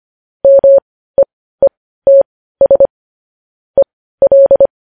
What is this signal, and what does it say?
Morse code, heard as a keyed tone, about 25 wpm, spaced as for 15 wpm, 550 Hz. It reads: MEETS EL
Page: news-headlines-in-morse-code-at-15-wpm